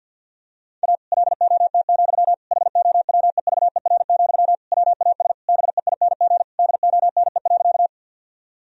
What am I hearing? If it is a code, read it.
Morse code, heard as a keyed tone, about 50 wpm, 700 Hz. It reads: A LOT, HOWEVER, WAS BEING DONE.
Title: Sentences from Top 1000 Words - Review Part 3 - 50wpm